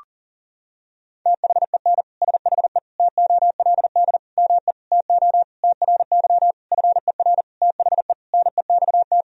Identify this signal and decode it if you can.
Morse code, heard as a keyed tone, about 40 wpm, 700 Hz. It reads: THEN SHE TOLD ME TO TRY FER THE NEXT
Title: Sentences from Top 400 Words and Abbreviations 40wpm